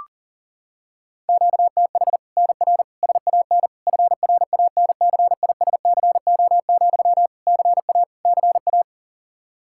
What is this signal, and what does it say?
Morse code, heard as a keyed tone, about 37 wpm, 700 Hz. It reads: QTH NR SAN FRANCISCO, CA CA
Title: Practice QSO - Exchange 1 - Beginning - 40wpm